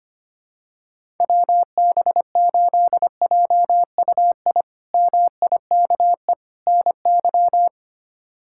Morse code, heard as a keyed tone, about 25 wpm, 700 Hz. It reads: WB8JUI MIKE NY
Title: K1USN SST QSO Practice - 25wpm